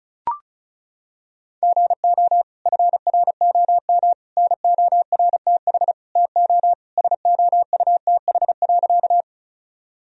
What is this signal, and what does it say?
Morse code, heard as a keyed tone, about 36 wpm, 700 Hz. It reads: GO FROM NORTH TO SOUTH.
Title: Sentences from Top 1000 Words - Review Part 2 - 35wpm